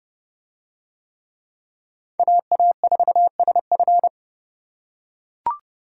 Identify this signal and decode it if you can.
Morse code, heard as a keyed tone, about 27 wpm, 700 Hz. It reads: AA4SF